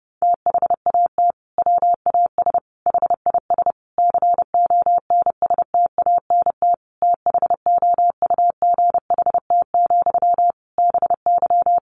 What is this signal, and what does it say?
Morse code, heard as a keyed tone, about 30 wpm, 700 Hz. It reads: THAT WAS HIS CONSTANT THOUGHT, BY
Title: Sentences from Top 1500 Words 30wpm